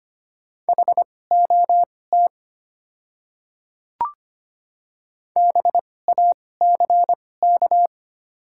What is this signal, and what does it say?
Morse code, heard as a keyed tone, about 25 wpm, spaced as for 18 wpm, 700 Hz. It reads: HOT BACK
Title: 2x Character Spacing - Top 300 Words - 25wpm